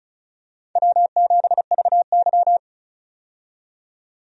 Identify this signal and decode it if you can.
Morse code, heard as a keyed tone, about 36 wpm, 700 Hz. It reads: W7VY
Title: USA Call Signs - 4 Characters - 35wpm